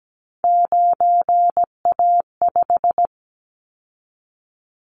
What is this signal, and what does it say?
Morse code, heard as a keyed tone, about 17 wpm, 700 Hz. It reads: 9A5
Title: Callsign Prefixes for Top 50 DXCC - 17wpm